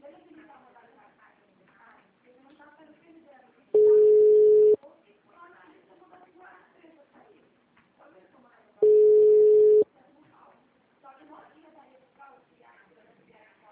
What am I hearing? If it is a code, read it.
Ringback tone (Europe, 425 Hz), with speech.
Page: CallCenter 2